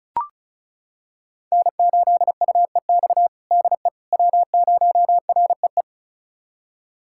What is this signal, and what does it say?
Morse code, heard as a keyed tone, about 34 wpm, 700 Hz. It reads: N8UEX DE W0REE
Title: USA Callsign DE Callsign - 5 Characters - 35wpm